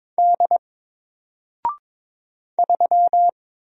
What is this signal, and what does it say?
Morse code, heard as a keyed tone, about 22 wpm, 700 Hz. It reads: D 3
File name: Single-Letter-Number-22wpm